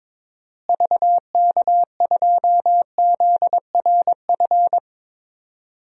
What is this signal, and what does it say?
Morse code, heard as a keyed tone, about 22 wpm, 700 Hz. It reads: VK2ZRF
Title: DXCC-100 Call Signs - 6 Characters - Repeated 2x - 22wpm